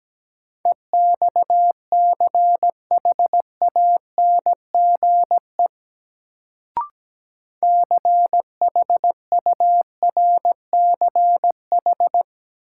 Morse code, heard as a keyed tone, about 17 wpm, 700 Hz. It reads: EXCHANGE CHURCH
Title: Sets of 5 Words - New Word Drill - 17wpm